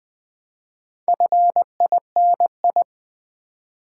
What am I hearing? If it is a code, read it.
Morse code, heard as a keyed tone, about 20 wpm, 700 Hz. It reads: FINI
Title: Build A Word - 2 Letters At A Time - 6-Letter Words - 20wpm